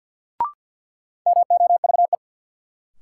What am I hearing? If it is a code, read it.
Morse code, heard as a keyed tone, about 53 wpm, 700 Hz. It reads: MOVE
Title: Sentences from Top 200 Words - New Word Drill - 50wpm